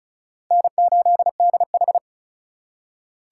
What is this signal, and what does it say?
Morse code, heard as a keyed tone, about 34 wpm, 700 Hz. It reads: N8DH
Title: USA Call Signs - 6x Character Spacing - 35wpm